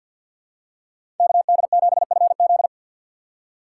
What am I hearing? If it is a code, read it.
Morse code, heard as a keyed tone, about 54 wpm, 700 Hz. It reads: KD7FZ
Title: USA Call Signs - 5 Characters - 8x Character Spacing - 50wpm